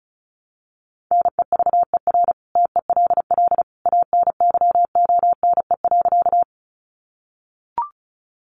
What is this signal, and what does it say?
Morse code, heard as a keyed tone, about 34 wpm, 700 Hz. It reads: NEVER TELL ANYONE.
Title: Sentences from Top 900 Words 35wpm